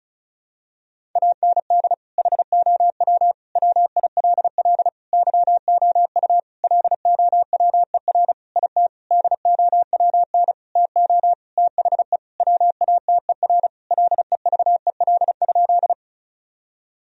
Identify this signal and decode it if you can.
Morse code, heard as a keyed tone, about 34 wpm, 700 Hz. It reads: AND HOW WILL YOU LOWER IT DOWN TO THE WATER LEVEL?
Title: Sentences from Top 1000 Words - Review Part 4 - 35wpm